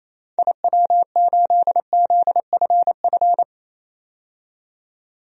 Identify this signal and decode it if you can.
Morse code, heard as a keyed tone, about 28 wpm, 700 Hz. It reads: IW8ZFF
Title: DXCC-100 Call Signs - 6 Characters - 28wpm